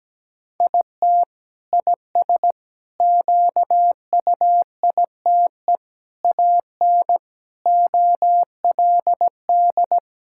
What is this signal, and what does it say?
Morse code, heard as a keyed tone, about 17 wpm, 700 Hz. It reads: IT IS QUITE AN OLD
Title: Sentences from Top 1000 Words - Review Part 5 - 17wpm